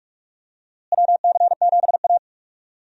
Morse code, heard as a keyed tone, about 46 wpm, 700 Hz. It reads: WC7A
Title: USA Call Signs - 4 Characters - Repeated 2x - 45wpm